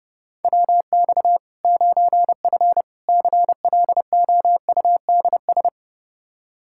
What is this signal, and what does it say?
Morse code, heard as a keyed tone, about 30 wpm, 700 Hz. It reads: WX 9F CLOUDS
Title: Practice QSO - Exchange 2 - Info - 30wpm